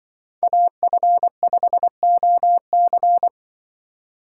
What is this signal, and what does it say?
Morse code, heard as a keyed tone, about 23 wpm, 700 Hz. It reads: AF5OC
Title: Speed Racing - USA 2x2 Call Signs 20wpm